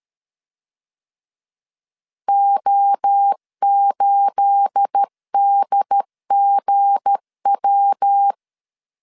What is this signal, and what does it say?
Morse code, heard as a keyed tone, about 13 wpm, 800 Hz. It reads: O8DGW
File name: CW / DET / 1 9 1500C S C 12wpm 3811